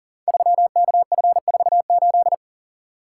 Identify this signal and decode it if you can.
Morse code, heard as a keyed tone, about 38 wpm, 700 Hz. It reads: 3KF48
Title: Code Groups - 5 Letter:Numbers - 40wpm